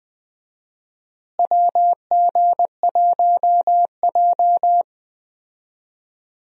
Morse code, heard as a keyed tone, about 20 wpm, 700 Hz. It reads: WG1J
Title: USA Call Signs - 4 Characters - Repeated 3x - 20wpm